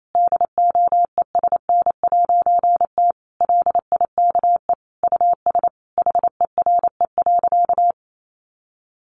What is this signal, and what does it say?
Morse code, heard as a keyed tone, about 28 wpm, 700 Hz. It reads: DOESN'T LIKE US HERE.
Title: A Sound Of Thunder 28wpm